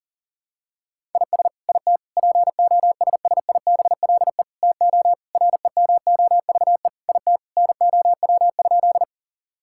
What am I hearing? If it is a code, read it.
Morse code, heard as a keyed tone, about 40 wpm, 700 Hz. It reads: IS IT POSSIBLE TO REMOVE IT NOW?
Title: Sentences from Top 1000 Words - Review Part 5 - 40wpm